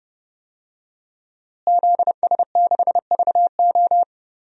Morse code, heard as a keyed tone, about 31 wpm, 700 Hz. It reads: ZS6VO
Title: DXCC-100 Call Signs - 5 Characters - Repeated 2x - 30wpm